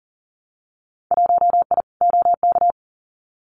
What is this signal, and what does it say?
Morse code, heard as a keyed tone, about 41 wpm, 700 Hz. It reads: 1I OK
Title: Winter Field Day Exchanges Encore 40wpm